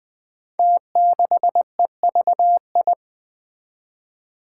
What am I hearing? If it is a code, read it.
Morse code, heard as a keyed tone, about 20 wpm, 700 Hz. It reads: T6EVI
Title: Five Challenging Letter-Number Combinations 20wpm